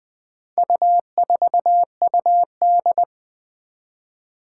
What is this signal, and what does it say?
Morse code, heard as a keyed tone, about 20 wpm, 700 Hz. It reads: U4UD
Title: Four Challenging Letter-Number Combinations 20wpm